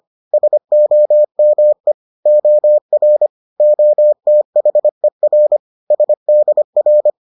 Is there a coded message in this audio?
Yes: Morse code from 0.3 to 7.1 s, heard as a keyed tone, about 25 wpm, 600 Hz. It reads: SOME OR OTHER SDR